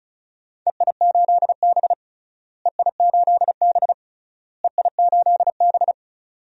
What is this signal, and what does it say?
Morse code, heard as a keyed tone, about 34 wpm, 700 Hz. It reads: EI8B EI8B EI8B
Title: DXCC-100 Call Signs - 4 Characters - Repeated 3x - 35wpm